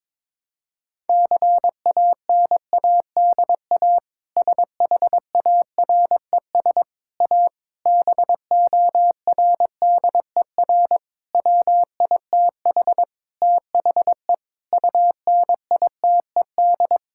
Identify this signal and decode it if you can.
Morse code, heard as a keyed tone, about 22 wpm, 700 Hz. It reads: CANADA SHARES A BORDER WITH THE UNITED
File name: practice / 22wpm / Sentences with 4-Word Phrases - 22wpm